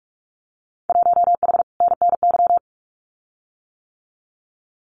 Morse code, heard as a keyed tone, about 44 wpm, 700 Hz. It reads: 1H NNY
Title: Winter Field Day Exchanges 45wpm